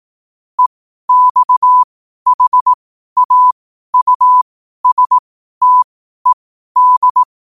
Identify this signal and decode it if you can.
Morse code, heard as a keyed tone, about 18 wpm, spaced as for 12 wpm, 1000 Hz. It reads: EXHAUSTED
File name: qotdfiles / 12wpm